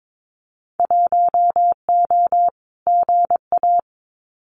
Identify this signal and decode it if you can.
Morse code, heard as a keyed tone, about 22 wpm, 700 Hz. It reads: 1O GA
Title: Winter Field Day Exchanges 22wpm